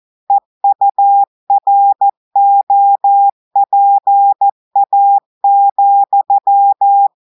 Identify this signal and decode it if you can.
Morse code, heard as a keyed tone, about 14 wpm, 800 Hz. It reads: EUROPA,
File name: audios / portugais / portugais texte en clair 1 vitesse 14 mots minutes